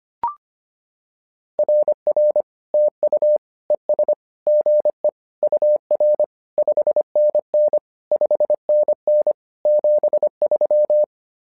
Morse code, heard as a keyed tone, about 25 wpm, 600 Hz. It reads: RR TU ES GE UR 5NN 5NN 73
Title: SOTA QSO Practice 25wpm